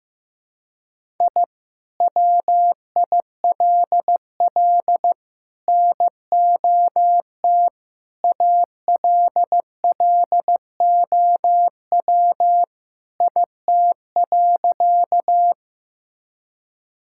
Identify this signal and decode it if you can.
Morse code, heard as a keyed tone, about 15 wpm, 700 Hz. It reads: I WILL NOT ALLOW IT.
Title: Sentences from Top 1000 Words - Review Part 5 - 15wpm